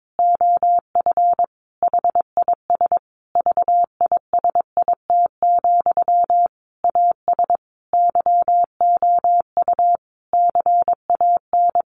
Morse code, heard as a keyed tone, about 22 wpm, 700 Hz. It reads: OF HIS VISIT, AS YOU CAN
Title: Sentences from Top 1400 Words 22wpm